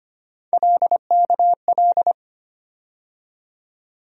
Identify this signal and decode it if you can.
Morse code, heard as a keyed tone, about 25 wpm, 700 Hz. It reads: LKL
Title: USA Call Sign Postfixes - 3 Letters 25wpm